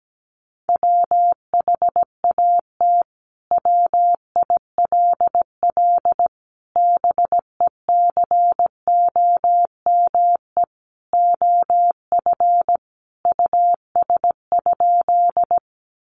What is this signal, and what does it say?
Morse code, heard as a keyed tone, about 17 wpm, 700 Hz. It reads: WHAT WILL BECOME OF US?
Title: Sentences from Top 600 Words 17wpm